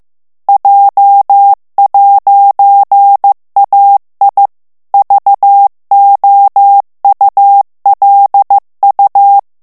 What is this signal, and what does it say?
Morse code, heard as a keyed tone, about 15 wpm, 800 Hz. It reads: J'AI VOULU
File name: FRANCAIS TEXTE CLAIR 2 VITESSE 14 MOTS MINUTES